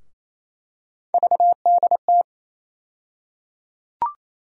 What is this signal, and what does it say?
Morse code, heard as a keyed tone, about 28 wpm, 700 Hz. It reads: VDT
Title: USA Call Sign Postfixes - 3 Letters 28wpm